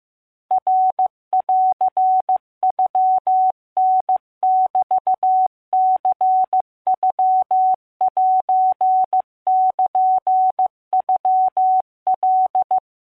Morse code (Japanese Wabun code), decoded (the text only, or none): ナンノタメニノセルノカ